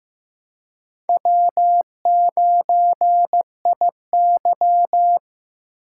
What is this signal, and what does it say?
Morse code, heard as a keyed tone, about 18 wpm, 700 Hz. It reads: W9IY
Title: Speed Racing - USA 1x2 Call Signs 15wpm